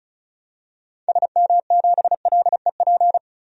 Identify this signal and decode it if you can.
Morse code, heard as a keyed tone, about 36 wpm, 700 Hz. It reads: SM7LEP
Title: DXCC-100 Call Signs - 6 Characters - 35wpm